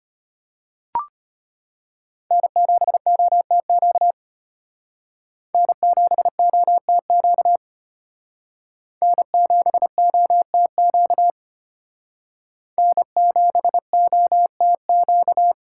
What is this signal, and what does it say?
Morse code, heard as a keyed tone, about 33 wpm, 700 Hz. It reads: N7OTQ N7OTQ N7OTQ N7OTQ